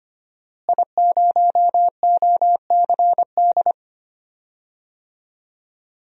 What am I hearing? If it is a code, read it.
Morse code, heard as a keyed tone, about 25 wpm, 700 Hz. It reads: I0OCD
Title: DXCC-100 Call Signs - 5 Characters - Repeated 2x - 25wpm